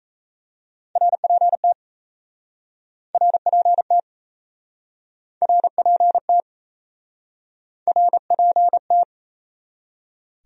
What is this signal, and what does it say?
Morse code, heard as a keyed tone, about 38 wpm, 700 Hz. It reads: RPT RPT RPT RPT